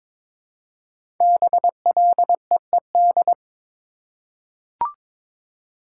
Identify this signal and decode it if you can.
Morse code, heard as a keyed tone, about 22 wpm, 700 Hz. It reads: BLEED
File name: 5 Letter Words - First 3 Letters Given - 22wpm